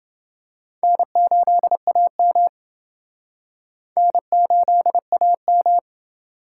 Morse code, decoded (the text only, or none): N8AM N8AM